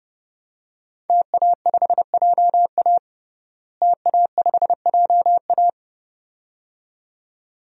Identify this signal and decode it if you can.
Morse code, heard as a keyed tone, about 30 wpm, 700 Hz. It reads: TA5JA TA5JA
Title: DXCC-100 Call Signs - 5 Characters - Repeated 2x - 30wpm